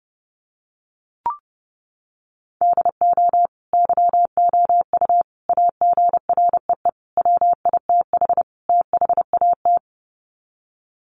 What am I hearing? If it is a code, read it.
Morse code, heard as a keyed tone, about 28 wpm, 700 Hz. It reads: DO YOU AGREE WITH THAT
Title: Sets of 5 Words Encore 30wpm